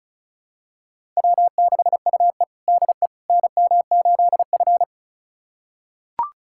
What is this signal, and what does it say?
Morse code, heard as a keyed tone, about 36 wpm, 700 Hz. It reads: W6UE DE NM8F
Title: USA Callsign DE Callsign - 4 Characters - 35wpm